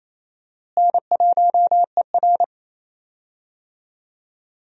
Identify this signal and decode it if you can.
Morse code, heard as a keyed tone, about 28 wpm, 700 Hz. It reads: N1ER
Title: USA Call Signs - 4 Characters - Repeated 3x - 28wpm